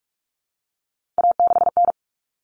Morse code, heard as a keyed tone, about 44 wpm, 700 Hz. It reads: A6N
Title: All Cut Numbers - 3-Digit 45wpm